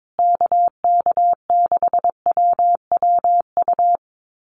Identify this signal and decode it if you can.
Morse code, heard as a keyed tone, about 22 wpm, 700 Hz. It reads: KK6WWU
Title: USA Call Signs Repeated 3x 22wpm